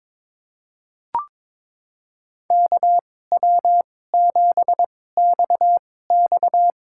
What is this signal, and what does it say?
Morse code, heard as a keyed tone, about 22 wpm, spaced as for 16 wpm, 700 Hz. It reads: KW7XX
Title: USA Call Signs - 5 Characters - 2x Character Spacing - 22wpm